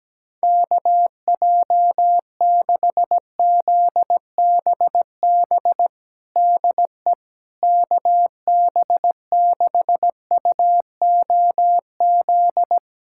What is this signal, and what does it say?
Morse code, heard as a keyed tone, about 17 wpm, 700 Hz. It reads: KJ6ZBB DE KB6UOZ
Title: USA Callsign DE Callsign 17wpm